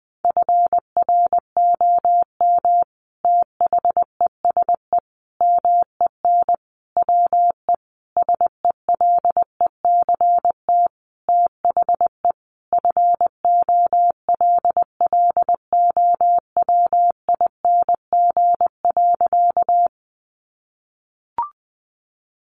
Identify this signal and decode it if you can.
Morse code, heard as a keyed tone, about 20 wpm, 700 Hz. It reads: FROM THESE MEN WE SELECT THE FOLLOWING.
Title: Sentences from Top 1300 Words 20wpm